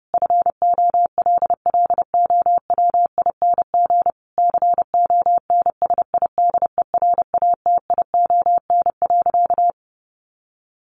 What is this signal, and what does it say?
Morse code, heard as a keyed tone, about 30 wpm, 700 Hz. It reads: FOLLOWING CONSIDERATION.
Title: Sentences from Top 1500 Words 30wpm